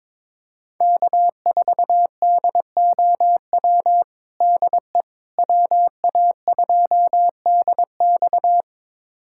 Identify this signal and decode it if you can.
Morse code, heard as a keyed tone, about 22 wpm, 700 Hz. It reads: K4DOW DE WA2DX
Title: USA Callsign DE Callsign - 5 Characters - 22wpm